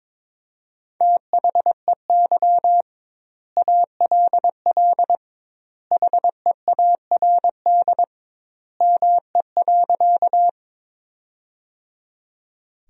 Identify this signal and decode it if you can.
Morse code, heard as a keyed tone, about 21 wpm, 700 Hz. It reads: THEY ALL HEARD ME.
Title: 2x Word-Spacing - Sentences from Top 400 Words - 22wpm